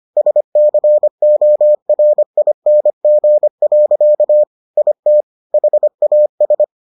Morse code, heard as a keyed tone, about 25 wpm, 600 Hz. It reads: SCORING. IT HAS